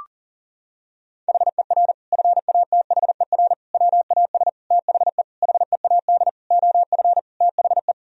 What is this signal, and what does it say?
Morse code, heard as a keyed tone, about 40 wpm, 700 Hz. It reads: HER FATHER WAS THE HEAD OF THE
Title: Sentences from Top 1000 Words - Review Part 4 - 40wpm